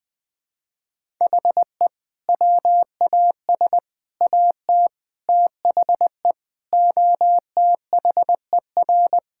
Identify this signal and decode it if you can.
Morse code, heard as a keyed tone, about 20 wpm, 700 Hz. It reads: HE WAS AT THE OTHER
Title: Sentences with 5-Word Phrases - Spoken Beginning - 20wpm